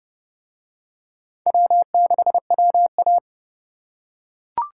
Morse code, heard as a keyed tone, about 30 wpm, 700 Hz. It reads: W6WA